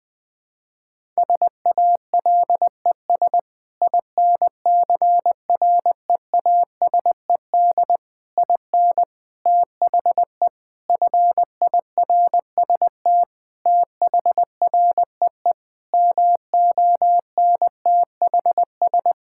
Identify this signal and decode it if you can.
Morse code, heard as a keyed tone, about 20 wpm, 700 Hz. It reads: SALES INCREASED IN THE FIRST THREE MONTHS
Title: Sentences with 8-Word Phrases - 20wpm